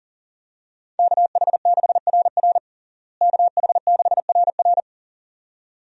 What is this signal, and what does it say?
Morse code, heard as a keyed tone, about 37 wpm, 700 Hz. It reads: KH6RR KH6RR
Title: USA Call Signs - 5 Characters - Repeated 2x - 40wpm